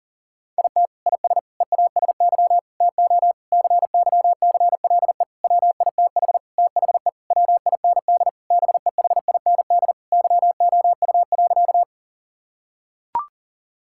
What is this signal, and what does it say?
Morse code, heard as a keyed tone, about 40 wpm, 700 Hz. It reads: IT IS EASY TO CYCLE WITH THE WIND BEHIND YOU.
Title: Sentences from Top 1000 Words - Review Part 6 - 40wpm